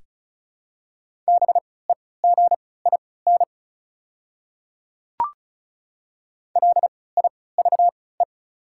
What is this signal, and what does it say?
Morse code, heard as a keyed tone, about 36 wpm, spaced as for 20 wpm, 700 Hz. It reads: BEGIN LIVE
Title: 3x Character Spacing - Top 300 Words - 35wpm